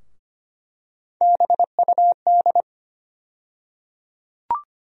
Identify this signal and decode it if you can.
Morse code, heard as a keyed tone, about 25 wpm, 700 Hz. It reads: BUD
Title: USA Call Sign Postfixes - 3 Letters 25wpm